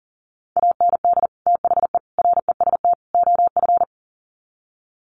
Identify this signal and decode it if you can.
Morse code, heard as a keyed tone, about 40 wpm, 700 Hz. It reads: AND THE REST OF
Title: Sets of 4 Words 40wpm